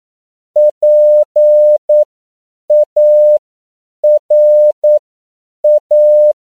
Morse code, heard as a keyed tone, about 9 wpm, spaced as for 7 wpm, 600 Hz. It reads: PARA
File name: CW / 10PPM